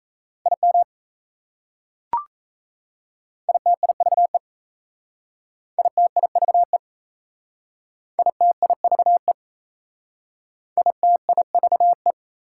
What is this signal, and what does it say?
Morse code, heard as a keyed tone, about 36 wpm, 700 Hz. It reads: IM ITIVE ITIVE ITIVE ITIVE